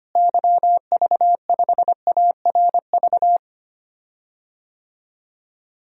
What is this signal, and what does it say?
Morse code, heard as a keyed tone, about 25 wpm, 700 Hz. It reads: YV5ARV
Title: DXCC-100 Call Signs - 6 Characters - Repeated 3x - 25wpm